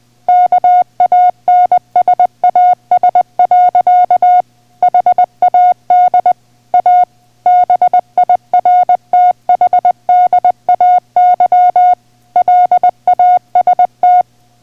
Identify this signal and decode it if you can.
Morse code, heard as a keyed tone, about 20 wpm, 720 Hz. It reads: KANSAS. HAD A BIRTHDAY LAST